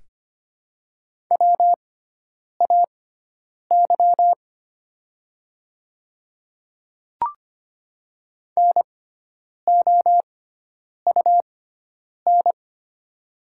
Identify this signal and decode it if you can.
Morse code, heard as a keyed tone, about 25 wpm, spaced as for 9 wpm, 700 Hz. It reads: WAY NOUN
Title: 6x Character Spacing - Top 500 Words - 25wpm